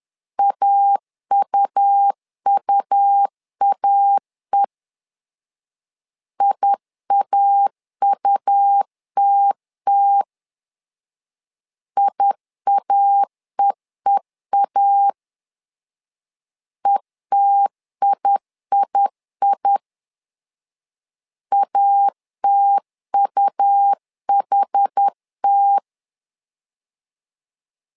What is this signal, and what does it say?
Morse code, heard as a keyed tone, about 10 wpm, 800 Hz. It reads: AUUAE IAUTT IAEEA ETIII ATUHT
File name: LEZIONI / 2 UHDN / 2 UH 10wpm 1714